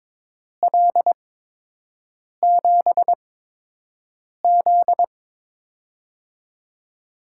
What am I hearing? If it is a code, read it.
Morse code, heard as a keyed tone, about 22 wpm, spaced as for 6 wpm, 700 Hz. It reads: L7Z
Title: USA Call Signs - 4 Characters - 8x Character Spacing - 22wpm